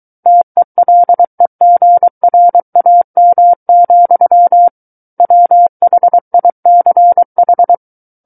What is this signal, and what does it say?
Morse code, heard as a keyed tone, about 23 wpm, 700 Hz. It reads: TELEGRAM, WHICH
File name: Book One / 14-Chapter 14 - 23 WPM